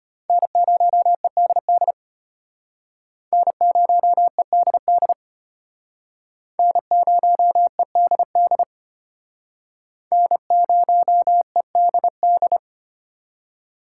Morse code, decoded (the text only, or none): N0EDD N0EDD N0EDD N0EDD